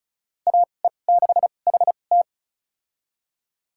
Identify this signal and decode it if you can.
Morse code, heard as a keyed tone, about 36 wpm, spaced as for 25 wpm, 700 Hz. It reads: AE6HT
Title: USA Call Signs - 5 Characters - 2x Character Spacing - 35wpm